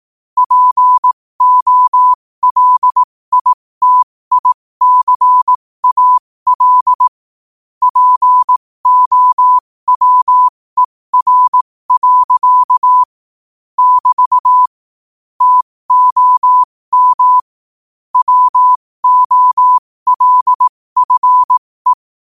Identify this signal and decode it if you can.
Morse code, heard as a keyed tone, about 18 wpm, spaced as for 15 wpm, 1000 Hz. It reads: POLITICAL POWER. <BT> TOM WOLFE
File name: qotdfiles / 15wpm